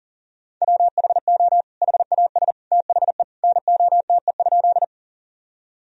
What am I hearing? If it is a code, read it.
Morse code, heard as a keyed tone, about 40 wpm, 700 Hz. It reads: WHO HAS THE NOTE?
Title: Sentences from Top 1000 Words - Review Part 2 - 40wpm